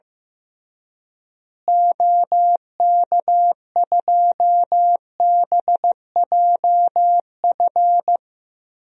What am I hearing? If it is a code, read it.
Morse code, heard as a keyed tone, about 15 wpm, 700 Hz. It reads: OK2BJF
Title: DXCC-100 Call Signs - 6 Characters - Repeated 3x - 15wpm